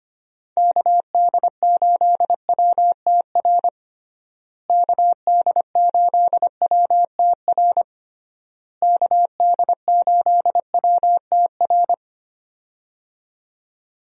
Morse code, heard as a keyed tone, about 25 wpm, 700 Hz. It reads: KD8WTR KD8WTR KD8WTR
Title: USA Call Signs - 6 Characters - Repeated 3x - 25wpm